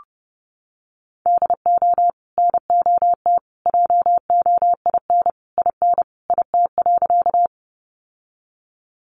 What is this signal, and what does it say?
Morse code, heard as a keyed tone, about 30 wpm, 700 Hz. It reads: DO NOT JOIN IN IT.
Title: Sentences from Top 1400 Words 30wpm